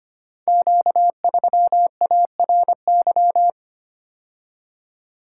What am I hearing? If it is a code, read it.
Morse code, heard as a keyed tone, about 25 wpm, 700 Hz. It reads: Q3ARY
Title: Code Groups - 5 Letter:Numbers - 25wpm